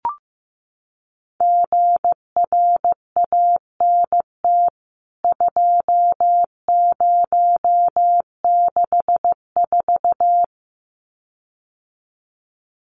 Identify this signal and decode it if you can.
Morse code, heard as a keyed tone, about 15 wpm, 700 Hz. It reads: GRANT 2064
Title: CWT Exchange 15wpm